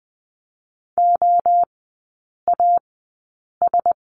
Morse code, heard as a keyed tone, about 20 wpm, spaced as for 8 wpm, 690 Hz. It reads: OAS
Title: 060 Lesson 11e - Letter h - Three Characters 20wpm